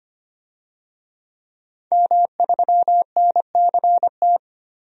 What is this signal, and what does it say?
Morse code, heard as a keyed tone, about 25 wpm, 700 Hz. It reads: M3NCT
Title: DXCC-100 Call Signs - 5 Characters - 25wpm